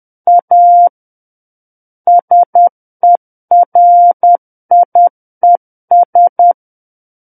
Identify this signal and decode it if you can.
Morse code, heard as a keyed tone, about 10 wpm, 700 Hz. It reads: A SERIES
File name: Book One / 01-Chapter 1 - 10 WPM